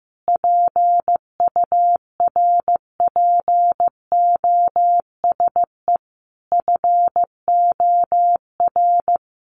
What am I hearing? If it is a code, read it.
Morse code, heard as a keyed tone, about 15 wpm, 700 Hz. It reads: PURPOSE FOR
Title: Sentences from Top 800 Words 15wpm